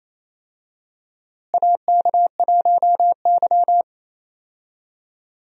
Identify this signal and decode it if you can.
Morse code, heard as a keyed tone, about 28 wpm, 700 Hz. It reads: AK1Y
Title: USA Call Signs - 4 Characters - Repeated 3x - 28wpm